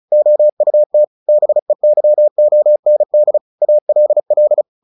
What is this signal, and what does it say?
Morse code, heard as a keyed tone, about 34 wpm, 600 Hz. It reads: OUT BEYOND ALL
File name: the old man and the sea / the old man and the sea 1 35wpm